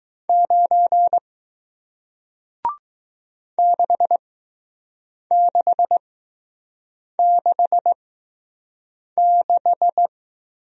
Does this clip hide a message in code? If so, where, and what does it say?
Yes: Morse code from 0.3 to 10.1 s, heard as a keyed tone, about 20 wpm, spaced as for 6 wpm, 700 Hz. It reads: 9 6666